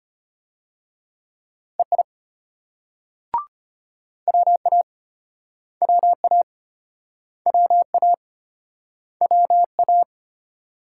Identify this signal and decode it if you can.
Morse code, heard as a keyed tone, about 32 wpm, 700 Hz. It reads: EI WA WA WA WA